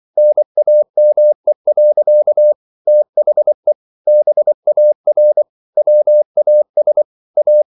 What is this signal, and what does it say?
Morse code, heard as a keyed tone, about 24 wpm, 600 Hz. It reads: NAME. THE BAR WAS A